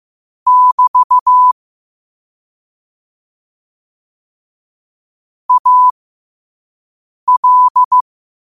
Morse code, heard as a keyed tone, about 15 wpm, spaced as for 5 wpm, 1000 Hz. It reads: <BT> AL
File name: qotdfiles / 5wpm